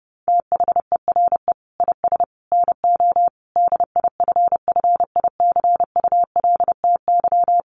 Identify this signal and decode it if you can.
Morse code, heard as a keyed tone, about 30 wpm, 700 Hz. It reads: THERE IS NO DIFFICULTY
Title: Sentences from Top 1500 Words 30wpm